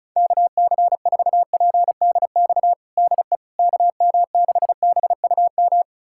Morse code, heard as a keyed tone, about 36 wpm, 700 Hz. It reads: KC4PDX DE KM6BUM
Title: USA Callsign DE Callsign 35wpm